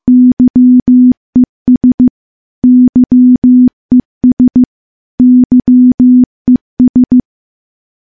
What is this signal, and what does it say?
Morse code, heard as a keyed tone, about 15 wpm, 250 Hz. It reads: YES YES YES
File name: morse / morse-training-15WPM